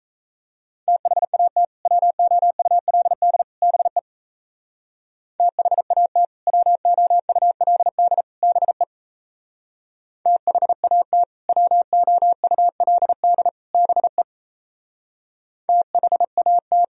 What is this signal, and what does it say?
Morse code, heard as a keyed tone, about 33 wpm, 700 Hz. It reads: THAT WOULD BE THAT WOULD BE THAT WOULD BE THAT
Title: Speed-Racing - Sets of 3 Words 28wpm